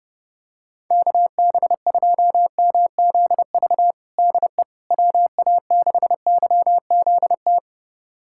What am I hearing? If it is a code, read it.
Morse code, heard as a keyed tone, about 30 wpm, 700 Hz. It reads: KB2MZV DE WA6YZT
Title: USA Callsign DE Callsign - 6 Characters - 30wpm